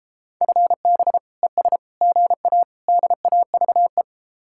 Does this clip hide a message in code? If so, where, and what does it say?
Yes: Morse code from 0.4 to 4.0 s, heard as a keyed tone, about 34 wpm, 700 Hz. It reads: FB ES GA DAVE